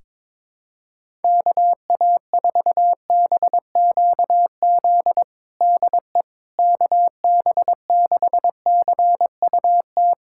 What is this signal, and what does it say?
Morse code, heard as a keyed tone, about 21 wpm, 700 Hz. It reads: KA4BQZ DE KB6CUT
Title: USA Callsign DE Callsign - 6 Characters - 22wpm